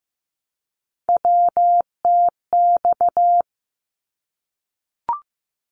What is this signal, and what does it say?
Morse code, heard as a keyed tone, about 15 wpm, 700 Hz. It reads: WTX
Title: ARRL Sections 15wpm